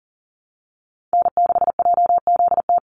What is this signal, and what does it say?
Morse code, heard as a keyed tone, about 38 wpm, 700 Hz. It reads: N6JZT
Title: NA SOTA Activator Callsigns 40wpm